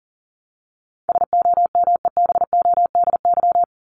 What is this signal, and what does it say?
Morse code, heard as a keyed tone, about 40 wpm, 700 Hz. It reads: SOMEBODY
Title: Top 1751-2000 Words 40wpm